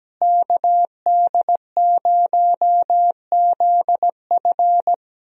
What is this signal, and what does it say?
Morse code, heard as a keyed tone, about 17 wpm, 700 Hz. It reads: KD0ZF
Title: USA Call Signs - 5 Characters - Repeated 3x - 17wpm